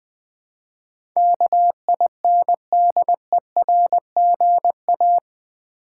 Morse code, heard as a keyed tone, about 20 wpm, 700 Hz. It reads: KINDERGA
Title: Build A Word - 4 Letters At A Time - 12-Letter Words - 20wpm